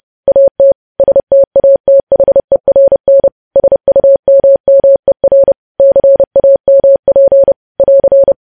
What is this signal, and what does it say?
Morse code, heard as a keyed tone, about 30 wpm, 550 Hz. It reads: AT STATHERN SUMMER CAMP <AR>
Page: News Headlines in Morse Code at 30 WPM